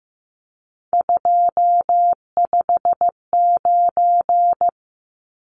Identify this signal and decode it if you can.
Morse code, heard as a keyed tone, about 15 wpm, 700 Hz. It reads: 259
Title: Numbers - 3-Digit 15wpm